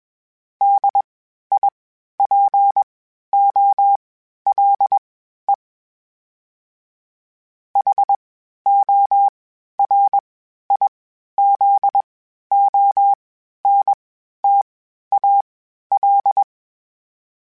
Morse code, heard as a keyed tone, about 21 wpm, spaced as for 12 wpm, 800 Hz. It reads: DIPOLE HORIZONTAL
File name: (spaced) / FromSMs spaced mp3s / Antennas sp@25